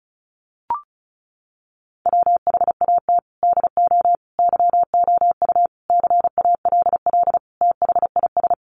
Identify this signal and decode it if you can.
Morse code, heard as a keyed tone, about 34 wpm, 700 Hz. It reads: WHAT DO YOU CALL THIS
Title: Sentences from Top 1400 Words 35wpm